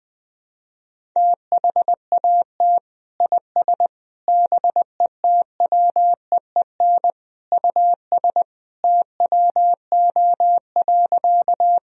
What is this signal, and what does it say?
Morse code, heard as a keyed tone, about 20 wpm, 700 Hz. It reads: THAT IS BETWEEN US TWO.
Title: Sentences from Top 1000 Words - Review Part 1 - 20wpm